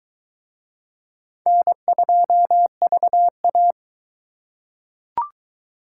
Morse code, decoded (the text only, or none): N2VA